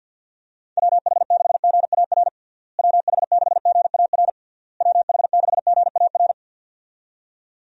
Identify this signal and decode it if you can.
Morse code, heard as a keyed tone, about 50 wpm, 700 Hz. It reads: WH6GAR WH6GAR WH6GAR
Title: USA Call Signs - 6 Characters - Repeated 3x - 50wpm